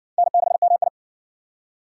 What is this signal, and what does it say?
Morse code, heard as a keyed tone, about 59 wpm, 700 Hz. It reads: N6MI